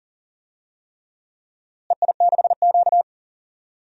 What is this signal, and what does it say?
Morse code, heard as a keyed tone, about 39 wpm, 700 Hz. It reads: EI6Q
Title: DXCC-100 Call Signs - 4 Characters - Repeated 3x - 40wpm